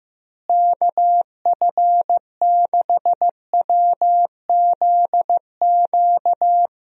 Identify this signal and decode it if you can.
Morse code, heard as a keyed tone, about 15 wpm, 700 Hz. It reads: KF6WZQ
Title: USA Call Signs - 6 Characters - Repeated 3x - 15wpm